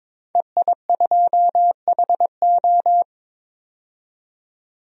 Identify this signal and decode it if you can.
Morse code, heard as a keyed tone, about 22 wpm, 700 Hz. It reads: EI2HO
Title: DXCC-100 Call Signs - 5 Characters - 22wpm